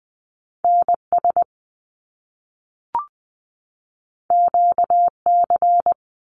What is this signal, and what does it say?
Morse code, heard as a keyed tone, about 20 wpm, 700 Hz. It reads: NS QC
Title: Canadian Provinces and Territories 20wpm